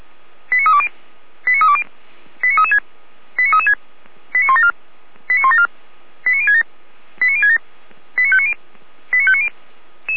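Five-tone selective call (ZVEI), heard as quick runs of five tones, tones of about 70 ms. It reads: 89320 89320 89336 89336 89174 89174 89067 89067 89590 89590
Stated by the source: ZVEI five-tone selective call received on an SDR.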